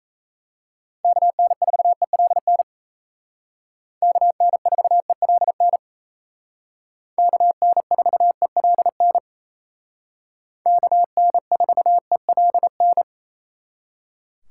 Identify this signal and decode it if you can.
Morse code, heard as a keyed tone, about 35 wpm, 700 Hz. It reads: KN4ELN KN4ELN KN4ELN KN4ELN